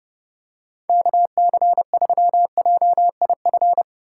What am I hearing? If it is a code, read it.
Morse code, heard as a keyed tone, about 30 wpm, 700 Hz. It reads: KC3JIF
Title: USA Call Signs - 6 Characters - 3x Character Spacing - 30wpm